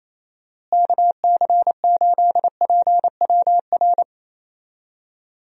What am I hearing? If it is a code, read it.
Morse code, heard as a keyed tone, about 28 wpm, 700 Hz. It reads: KC8PWR
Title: USA Call Signs - 6 Characters - Repeated 3x - 28wpm